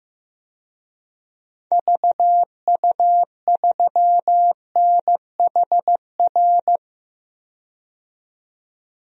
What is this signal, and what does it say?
Morse code, heard as a keyed tone, about 15 wpm, 700 Hz. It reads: VU3NHR
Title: DXCC-100 Call Signs - 6 Characters - 15wpm